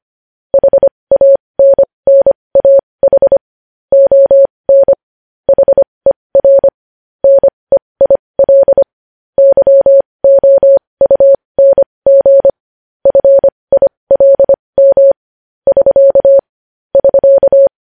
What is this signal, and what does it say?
Morse code, heard as a keyed tone, about 25 wpm, spaced as for 20 wpm, 550 Hz. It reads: HANNAH ON HER NEIL YOUNG FILM <SK> <SK>
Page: news-headlines-in-morse-code-at-20-wpm